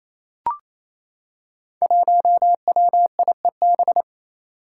Morse code, heard as a keyed tone, about 29 wpm, 700 Hz. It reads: 1WIEB
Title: Five Challenging Letter-Number Combinations 28wpm